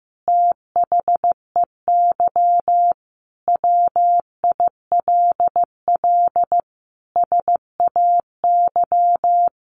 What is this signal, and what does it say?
Morse code, heard as a keyed tone, about 15 wpm, 700 Hz. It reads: THEY WILL SAY
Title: Sentences from Top 1500 Words 15wpm